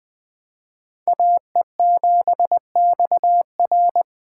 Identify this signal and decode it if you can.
Morse code, heard as a keyed tone, about 20 wpm, 700 Hz. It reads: AE7XR
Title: Speed Racing - USA 2x2 Call Signs 17wpm